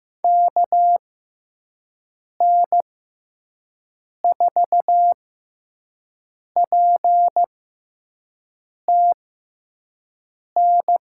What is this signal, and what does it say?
Morse code, heard as a keyed tone, about 15 wpm, spaced as for 5 wpm, 700 Hz. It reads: KN4PTN
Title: USA Call Signs - 6 Characters - 6x Character Spacing - 15wpm